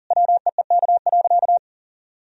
Morse code, heard as a keyed tone, about 40 wpm, 700 Hz. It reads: WEEK.
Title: Sentences from Top 1000 Words - Review Part 2 - 40wpm